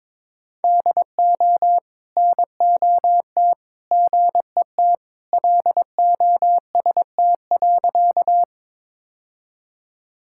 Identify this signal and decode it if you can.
Morse code, heard as a keyed tone, about 22 wpm, 700 Hz. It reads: DO NOT GET LOST.
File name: practice / 22wpm / Sentences with 3-Word Phrases - 22wpm